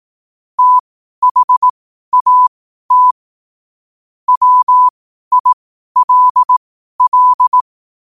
Morse code, heard as a keyed tone, about 18 wpm, spaced as for 12 wpm, 1000 Hz. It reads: THAT WILL
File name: qotdfiles / 12wpm